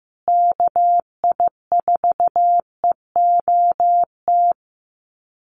Morse code, heard as a keyed tone, about 15 wpm, 700 Hz. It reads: KI4EOT
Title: USA Call Signs Repeated 3x 15wpm